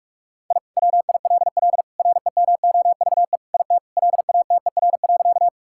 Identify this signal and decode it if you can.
Morse code, heard as a keyed tone, about 46 wpm, 700 Hz. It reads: I WILL REMOVE IT LATER.
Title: Sentences from Top 1000 Words - Review Part 5 - 45wpm